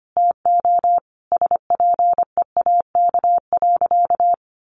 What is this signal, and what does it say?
Morse code, heard as a keyed tone, about 25 wpm, 700 Hz. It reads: TO SPEAK.
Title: Sentences from Top 1400 Words 25wpm